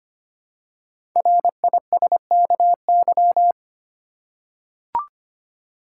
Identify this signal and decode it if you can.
Morse code, heard as a keyed tone, about 25 wpm, 700 Hz. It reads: RISKY
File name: 5 Letter Words - First 3 Letters Given - 25wpm